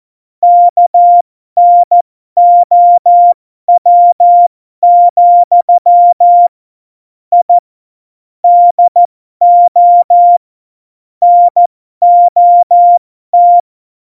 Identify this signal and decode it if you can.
Morse code, heard as a keyed tone, about 14 wpm, spaced as for 12 wpm, 700 Hz. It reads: KNOW, I DO NOT
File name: Morsekurs / Witze